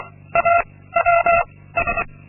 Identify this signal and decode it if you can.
Morse code, heard as a keyed tone, about 24 wpm, 680 Hz. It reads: A W S
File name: Linksmakalnis-2